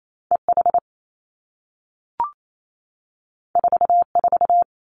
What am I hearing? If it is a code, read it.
Morse code, heard as a keyed tone, about 28 wpm, 700 Hz. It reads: E H 4 4
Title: Two Challenging Letter-Number Combinations 28wpm